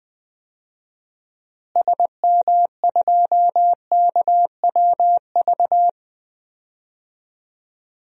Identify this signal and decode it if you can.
Morse code, heard as a keyed tone, about 20 wpm, 700 Hz. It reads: SM2KWV
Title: DXCC-100 Call Signs - 6 Characters - Repeated 2x - 20wpm